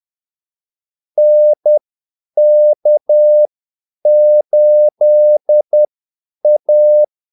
Morse code, heard as a keyed tone, about 10 wpm, spaced as for 8 wpm, 600 Hz. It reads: NK8A
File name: Callsigns-10-8wpm